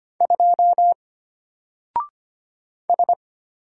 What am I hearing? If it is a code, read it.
Morse code, heard as a keyed tone, about 25 wpm, 700 Hz. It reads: 2 S
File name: Single-Letter-Number-25wpm